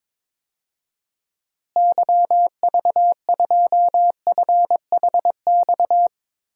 Morse code, heard as a keyed tone, about 21 wpm, 700 Hz. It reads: YV2FHX
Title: DXCC-100 Call Signs - 6 Characters - Repeated 2x - 22wpm